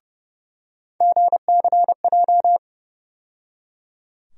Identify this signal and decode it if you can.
Morse code, heard as a keyed tone, about 30 wpm, 700 Hz. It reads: GCJ
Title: USA Call Sign Postfixes - 3 Letters 30wpm